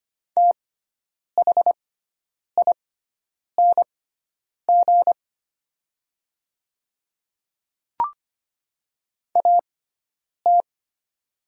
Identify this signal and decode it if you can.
Morse code, heard as a keyed tone, about 25 wpm, spaced as for 9 wpm, 700 Hz. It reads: THING AT
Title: 6x Character Spacing - Top 100 Words - 25wpm